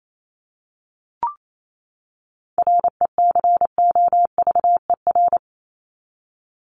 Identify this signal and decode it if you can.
Morse code, heard as a keyed tone, about 28 wpm, 700 Hz. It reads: RECOVER
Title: Top 1751-2000 Words 28wpm